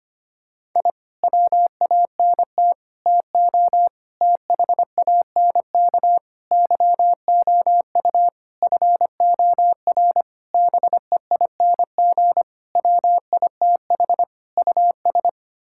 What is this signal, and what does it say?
Morse code, heard as a keyed tone, about 25 wpm, 700 Hz. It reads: I WANT TO THANK YOU FOR BEING WITH US
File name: Sets of 9 Words - 25wpm